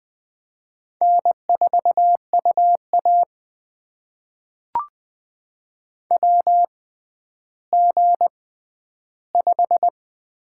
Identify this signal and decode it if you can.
Morse code, heard as a keyed tone, about 20 wpm, spaced as for 11 wpm, 700 Hz. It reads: N4UA WG5
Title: USA Call Signs - 4 Characters - 6x Character Spacing - 20wpm